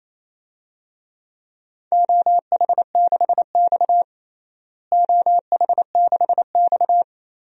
Morse code, heard as a keyed tone, about 27 wpm, 700 Hz. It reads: OH6X OH6X
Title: DXCC-100 Call Signs - 4 Characters - Repeated 3x - 28wpm